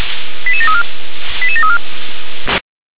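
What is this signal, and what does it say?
Five-tone selective call (ZVEI), heard as quick runs of five tones, tones of about 70 ms. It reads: 88734 88734
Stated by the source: ZVEI five-tone selective call received with an SDR.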